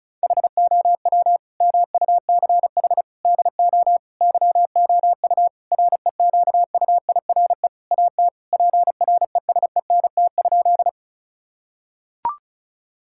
Morse code, heard as a keyed tone, about 34 wpm, 700 Hz. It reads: HOW MUCH DO YOU REQUIRE AT PRESENT?
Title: Sentences from Top 1000 Words - Review Part 6 - 35wpm